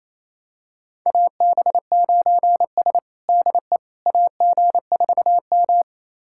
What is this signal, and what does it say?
Morse code, heard as a keyed tone, about 28 wpm, 700 Hz. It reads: AB9S DE AG4M
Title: USA Callsign DE Callsign - 4 Characters - 28wpm